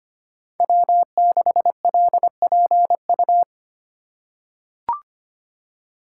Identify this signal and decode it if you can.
Morse code, heard as a keyed tone, about 25 wpm, 700 Hz. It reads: W6LPU
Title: USA Call Signs - 3x Character Spacing - 25wpm